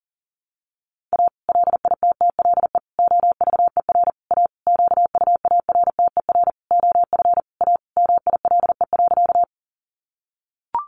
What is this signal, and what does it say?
Morse code, heard as a keyed tone, about 40 wpm, 700 Hz. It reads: A LITTLE OVER A QUARTER OF A MILE.
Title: Sentences from Top 1300 Words 40wpm